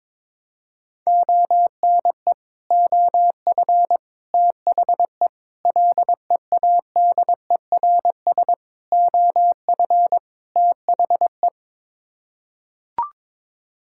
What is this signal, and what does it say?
Morse code, heard as a keyed tone, about 22 wpm, 700 Hz. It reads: ONE OF THE LEADERS OF THE
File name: practice / 22wpm / Sets of 6 Words - 22wpm